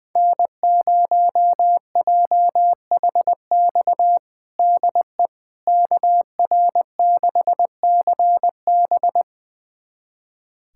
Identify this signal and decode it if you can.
Morse code, heard as a keyed tone, about 20 wpm, 700 Hz. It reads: N0JHX DE KR6CB
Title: USA Callsign DE Callsign - 5 Characters - 20wpm